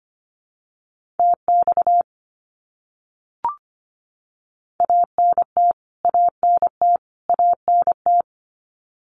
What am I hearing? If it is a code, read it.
Morse code, heard as a keyed tone, about 25 wpm, 700 Hz. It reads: TX ANT ANT ANT
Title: Abbreviations Repeated 3x - 25wpm